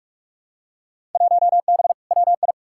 Morse code, heard as a keyed tone, about 46 wpm, 700 Hz. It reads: 1B WI
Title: Field Day Exchanges - 45wpm